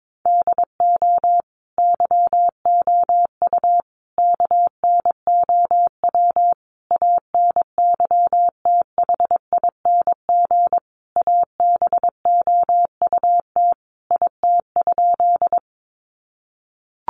Morse code, heard as a keyed tone, about 22 wpm, 700 Hz. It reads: DO YOU KNOW ANYTHING ABOUT IT?
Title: Sentences from Top 1000 Words 22wpm